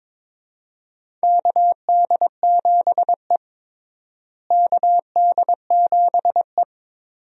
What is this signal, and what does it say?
Morse code, heard as a keyed tone, about 22 wpm, 700 Hz. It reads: KD7E KD7E
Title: USA Call Signs - 4 Characters - Repeated 2x - 22wpm